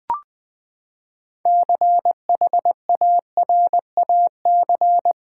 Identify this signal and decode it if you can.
Morse code, heard as a keyed tone, about 20 wpm, 700 Hz. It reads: CHARAC
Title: Build A Word - 6 Letters At A Time - 12-Letter Words - 20wpm